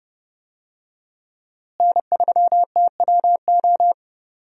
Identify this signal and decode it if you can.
Morse code, heard as a keyed tone, about 31 wpm, 700 Hz. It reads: N3TWO
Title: USA Call Signs - 5 Characters - Repeated 2x - 30wpm